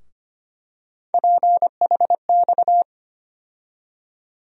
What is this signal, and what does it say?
Morse code, heard as a keyed tone, about 25 wpm, 700 Hz. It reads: PHX
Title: USA Call Sign Postfixes - 3 Letters 25wpm